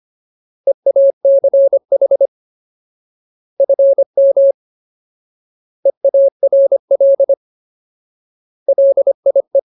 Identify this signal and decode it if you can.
Morse code, heard as a keyed tone, about 25 wpm, 550 Hz. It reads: EACH FM EARL LIE
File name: Ultimate Word List / 25WPM Ultimate